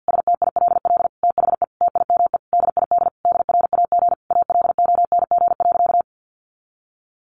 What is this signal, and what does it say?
Morse code, heard as a keyed tone, about 50 wpm, 700 Hz. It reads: STILL THE TIME DID DRAG ALONG.
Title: Sentences from Top 1500 Words 50wpm